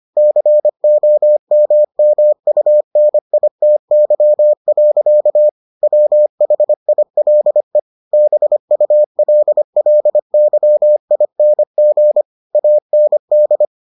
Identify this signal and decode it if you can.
Morse code, heard as a keyed tone, about 25 wpm, 600 Hz. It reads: COMMUNITY. WHILE BULLYING AND